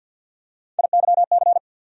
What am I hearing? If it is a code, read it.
Morse code, heard as a keyed tone, about 47 wpm, 700 Hz. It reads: IYC
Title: USA Call Sign Postfixes - 3 Letters 50wpm